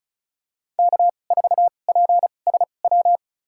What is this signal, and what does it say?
Morse code, heard as a keyed tone, about 34 wpm, spaced as for 25 wpm, 700 Hz. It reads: K4PSW
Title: USA Call Signs - 2x Character Spacing - 35wpm